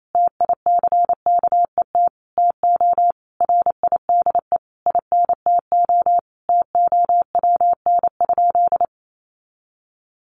Morse code, heard as a keyed tone, about 28 wpm, 700 Hz. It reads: TICKET TO RIDE INTO TOWN?
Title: Sentences from Top 1400 Words 28wpm